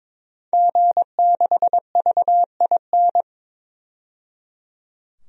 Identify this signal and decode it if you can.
Morse code, heard as a keyed tone, about 22 wpm, 700 Hz. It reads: G6VIN
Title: DXCC-100 Call Signs - 5 Characters - 22wpm